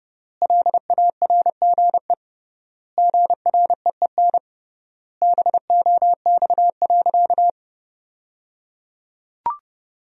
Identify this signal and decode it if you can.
Morse code, heard as a keyed tone, about 28 wpm, 700 Hz. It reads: LARGE GREEN BOX.
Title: 3x Word-Spacing - Sentences from Top 500 Words - 30wpm